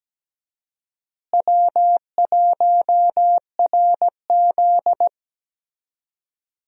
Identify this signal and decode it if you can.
Morse code, heard as a keyed tone, about 17 wpm, 700 Hz. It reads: W1RZ
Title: USA Call Signs - 4 Characters - Repeated 2x - 17wpm